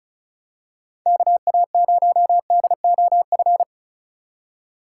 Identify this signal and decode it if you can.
Morse code, heard as a keyed tone, about 34 wpm, 700 Hz. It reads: KA0DOF
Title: USA Call Signs - 6 Characters - 4x Character Spacing - 35wpm